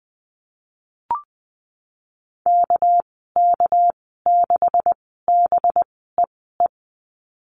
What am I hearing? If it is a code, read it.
Morse code, heard as a keyed tone, about 19 wpm, spaced as for 14 wpm, 700 Hz. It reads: KK6BEE
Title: 207 Lesson 33g - Letter j - Call Signs 20wpm